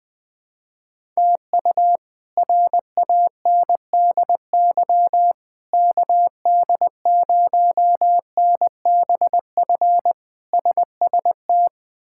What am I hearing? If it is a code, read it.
Morse code, heard as a keyed tone, about 20 wpm, 700 Hz. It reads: TU RANDY KD0NBF SST
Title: K1USN SST QSO Practice - 20wpm